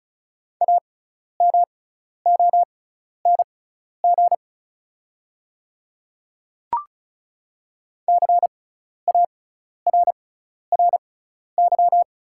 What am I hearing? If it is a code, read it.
Morse code, heard as a keyed tone, about 34 wpm, spaced as for 12 wpm, 690 Hz. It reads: AMONG CARRY
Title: 6x Character Spacing - Top 500 Words - 35wpm